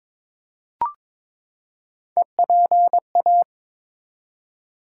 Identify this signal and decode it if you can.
Morse code, heard as a keyed tone, about 22 wpm, 700 Hz. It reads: EPA
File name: ARRL Sections v2 - 22wpm